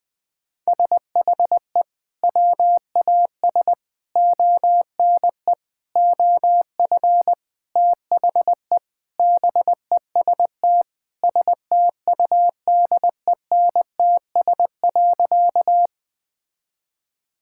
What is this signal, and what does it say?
Morse code, heard as a keyed tone, about 20 wpm, 700 Hz. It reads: SHE WAS ONE OF THE BEST STUDENTS.
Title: Sentences with 4-Word Phrases - 20wpm